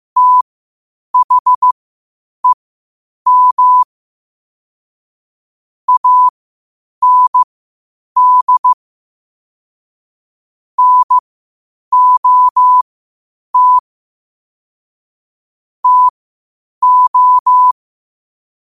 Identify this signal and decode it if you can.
Morse code, heard as a keyed tone, about 15 wpm, spaced as for 8 wpm, 1000 Hz. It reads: THEM AND NOT TO
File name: qotdfiles / 8wpm